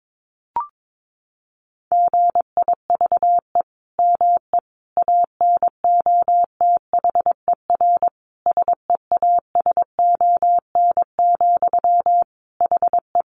Morse code, heard as a keyed tone, about 22 wpm, 700 Hz. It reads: GIVE ME ANOTHER SEASON, HE
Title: Sentences from Top 1400 Words 22wpm